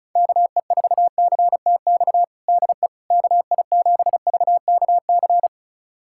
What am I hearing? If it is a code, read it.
Morse code, heard as a keyed tone, about 34 wpm, 700 Hz. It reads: KE4CTX DE KI7VKC
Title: USA Callsign DE Callsign - 6 Characters - 35wpm